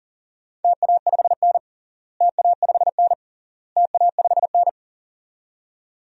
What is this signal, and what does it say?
Morse code, heard as a keyed tone, about 37 wpm, 700 Hz. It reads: TA5N TA5N TA5N
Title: DXCC-100 Call Signs - 4 Characters - Repeated 3x - 40wpm